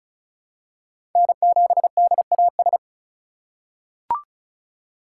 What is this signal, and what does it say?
Morse code, heard as a keyed tone, about 36 wpm, 700 Hz. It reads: N7DAS
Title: USA Call Signs - 5 Characters - 3x Character Spacing - 35wpm